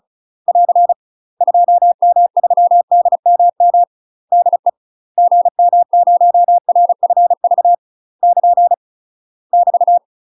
Morse code, heard as a keyed tone, about 36 wpm, 700 Hz. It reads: <AR> 2M3DMM DE GM0RFV <KN> <BT>
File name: UK MASTER QSOs TRAINING / 35WPM / UK MASTER QSOs 0032